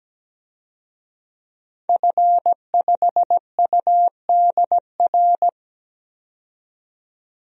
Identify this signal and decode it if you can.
Morse code, heard as a keyed tone, about 17 wpm, 700 Hz. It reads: F5UDR
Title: DXCC-100 Call Signs - 5 Characters - 17wpm